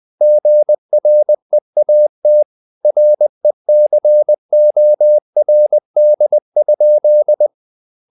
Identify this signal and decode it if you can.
Morse code, heard as a keyed tone, about 20 wpm, 600 Hz. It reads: GREAT RECORD?
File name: the old man and the sea / the old man and the sea part 1 20wpm